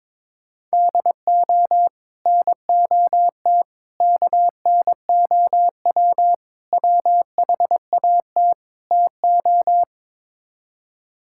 Morse code, heard as a keyed tone, about 22 wpm, 700 Hz. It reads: DO NOT KNOW WHAT TO
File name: practice / 22wpm / Build A Phrase - 5-Word Phrases - 22wpm